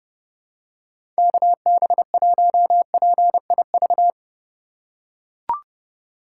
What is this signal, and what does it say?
Morse code, heard as a keyed tone, about 32 wpm, 700 Hz. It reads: KB1PIV
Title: USA Call Signs - 3x Character Spacing - 30wpm